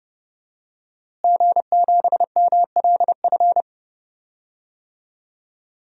Morse code, heard as a keyed tone, about 30 wpm, 700 Hz. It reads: G7MLF
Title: DXCC-100 Call Signs - 5 Characters - Repeated 3x - 30wpm